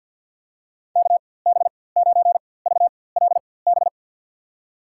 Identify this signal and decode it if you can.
Morse code, heard as a keyed tone, about 49 wpm, spaced as for 23 wpm, 700 Hz. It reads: KB9VLB
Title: USA Call Signs - 6 Characters - 4x Character Spacing - 50wpm